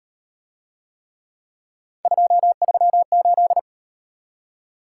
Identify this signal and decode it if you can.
Morse code, heard as a keyed tone, about 33 wpm, 700 Hz. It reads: 238